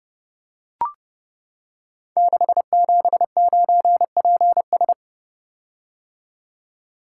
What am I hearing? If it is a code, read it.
Morse code, heard as a keyed tone, about 30 wpm, 700 Hz. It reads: 679PS
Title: Code Groups - 5 Letter:Numbers - 30wpm